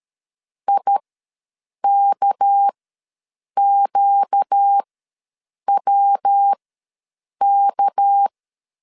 Morse code, heard as a keyed tone, about 13 wpm, spaced as for 7 wpm, 800 Hz. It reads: IKQWK
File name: LEZIONI / 7 GF9Z / 7 GF9 ws 12wpm 3344